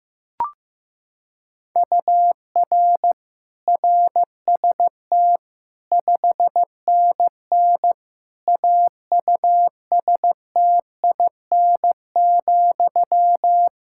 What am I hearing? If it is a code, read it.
Morse code, heard as a keyed tone, about 15 wpm, 700 Hz. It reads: UR RST 5NN AUSTIN,
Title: Practice QSO - Exchange 1 - Beginning - 15wpm